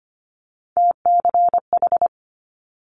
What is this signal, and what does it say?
Morse code, heard as a keyed tone, about 25 wpm, 700 Hz. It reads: TCH
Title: Multi-letter Phonemes Quick Fire 25wpm